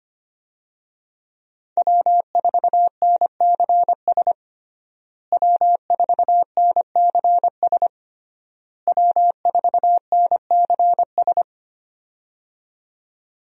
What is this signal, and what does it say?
Morse code, heard as a keyed tone, about 25 wpm, 700 Hz. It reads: W4NCS W4NCS W4NCS
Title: USA Call Signs - 5 Characters - Repeated 3x - 25wpm